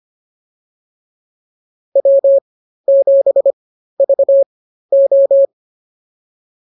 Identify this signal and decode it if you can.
Morse code, heard as a keyed tone, about 25 wpm, spaced as for 13 wpm, 550 Hz. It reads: W7VO